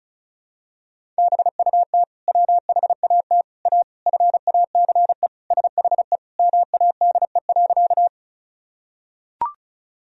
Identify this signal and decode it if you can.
Morse code, heard as a keyed tone, about 34 wpm, 700 Hz. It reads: BUT WHAT A FACE SHE MADE.
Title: Sentences from Top 1000 Words - Review Part 2 - 35wpm